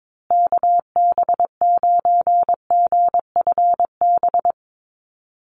Morse code, heard as a keyed tone, about 22 wpm, 700 Hz. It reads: KB9GFB
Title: USA Call Signs Repeated 2x 22wpm